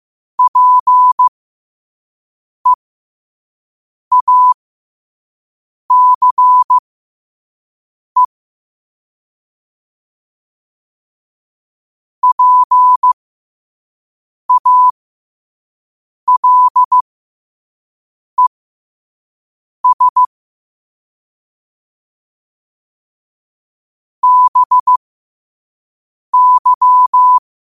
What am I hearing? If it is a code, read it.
Morse code, heard as a keyed tone, about 15 wpm, spaced as for 5 wpm, 1000 Hz. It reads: PEACE PALES BY
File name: qotdfiles / 5wpm